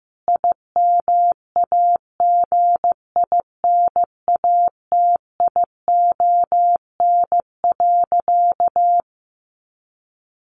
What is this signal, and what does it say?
Morse code, heard as a keyed tone, about 15 wpm, 700 Hz. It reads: IMAGINATION.
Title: Sentences from Top 1300 Words 15wpm